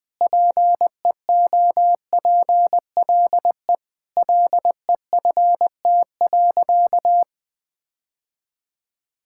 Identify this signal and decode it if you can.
Morse code, heard as a keyed tone, about 19 wpm, 700 Hz. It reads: PEOPLE LEFT.
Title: Sentences with 4-Word Phrases - 20wpm